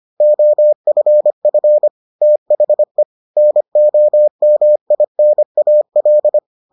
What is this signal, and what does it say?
Morse code, heard as a keyed tone, about 25 wpm, 600 Hz. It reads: OFF THE NOMINAL